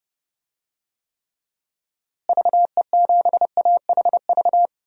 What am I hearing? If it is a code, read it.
Morse code, heard as a keyed tone, about 32 wpm, 700 Hz. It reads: VE7AHV
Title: DXCC-100 Call Signs - 6 Characters - Repeated 2x - 30wpm